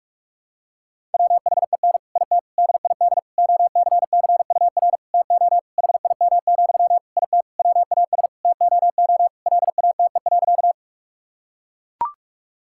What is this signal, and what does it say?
Morse code, heard as a keyed tone, about 41 wpm, 700 Hz. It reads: WHEN IT DID OCCUR TO HIM, IT WAS TOO LATE.
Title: Sentences from Top 1000 Words - Review Part 6 - 45wpm